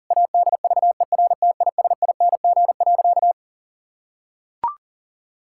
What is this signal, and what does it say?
Morse code, heard as a keyed tone, about 37 wpm, 700 Hz. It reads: ADVERTISING.
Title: Sentences from Top 1000 Words - Review Part 6 - 40wpm